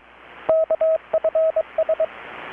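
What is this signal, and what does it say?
Morse code, heard as a keyed tone, about 23 wpm, 640 Hz, with radio noise. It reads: KFS
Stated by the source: Morse from KPH's Night of Nights XX maritime radiotelegraphy event, received in Chiapas, Mexico about 3,800 km away on a Kenwood R-600 with a grounded 10 m long-wire antenna.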